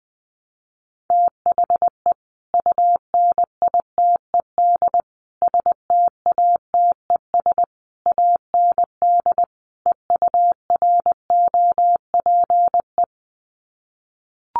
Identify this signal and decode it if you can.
Morse code, heard as a keyed tone, about 20 wpm, 700 Hz. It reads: THE UNITED STATES AND EUROPE
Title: Sets of 5 Words Encore 20wpm